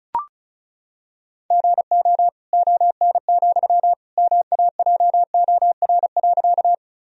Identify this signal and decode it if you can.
Morse code, heard as a keyed tone, about 36 wpm, 700 Hz. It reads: GO ON, MAJOR.
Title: Sentences from Top 1000 Words - Review Part 4 - 35wpm